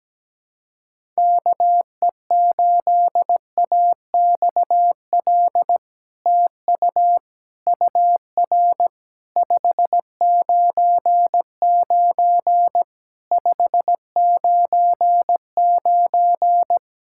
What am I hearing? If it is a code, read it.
Morse code, heard as a keyed tone, about 17 wpm, 700 Hz. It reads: KE8AXL TU UR 599 599
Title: SOTA QSO Practice 17wpm